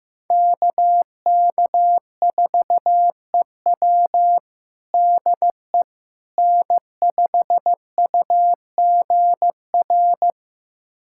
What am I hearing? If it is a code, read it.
Morse code, heard as a keyed tone, about 16 wpm, 700 Hz. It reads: KK4EW DE N5UGR
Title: USA Callsign DE Callsign - 5 Characters - 15wpm